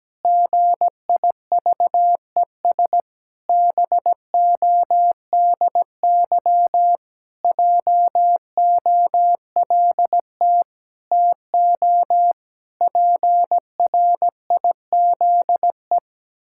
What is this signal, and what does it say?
Morse code, heard as a keyed tone, about 17 wpm, 700 Hz. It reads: GIVES BODY JOLT TO PRIZE
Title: Pangram Challenge v2 - 17wpm